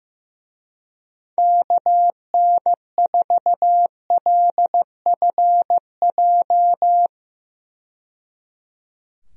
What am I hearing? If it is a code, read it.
Morse code, heard as a keyed tone, about 15 wpm, 700 Hz. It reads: KN4LFJ
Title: USA Call Signs - 6 Characters - 15wpm